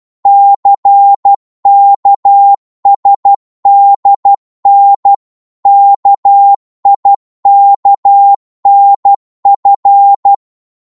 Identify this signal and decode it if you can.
Morse code, heard as a keyed tone, about 12 wpm, 800 Hz. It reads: CKSDN KIKNF
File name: Morse / ltr-12wpm